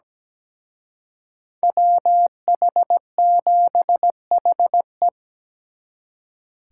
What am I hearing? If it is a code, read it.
Morse code, heard as a keyed tone, about 20 wpm, 700 Hz. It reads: WH7HE
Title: Speed Racing - USA 2x2 Call Signs 17wpm